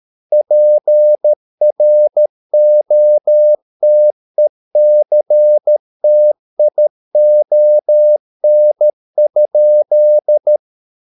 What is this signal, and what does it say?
Morse code, heard as a keyed tone, about 13 wpm, 600 Hz. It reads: PROTECTION?